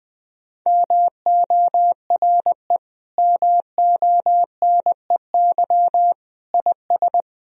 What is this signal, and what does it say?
Morse code, heard as a keyed tone, about 21 wpm, 700 Hz. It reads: MORE MONEY IS
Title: Sentences from Top 1000 Words - Review Part 3 - 20wpm